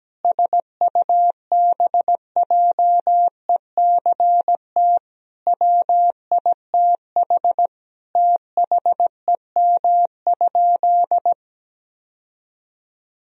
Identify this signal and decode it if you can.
Morse code, heard as a keyed tone, about 17 wpm, 700 Hz. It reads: SUBJECT WITH THEM?
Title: Sentences from Top 1000 Words - Review Part 1 - 17wpm